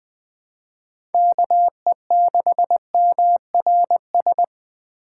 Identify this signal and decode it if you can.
Morse code, heard as a keyed tone, about 20 wpm, 700 Hz. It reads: KE6MRS
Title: USA Call Signs - 6 Characters - 8x Character Spacing - 20wpm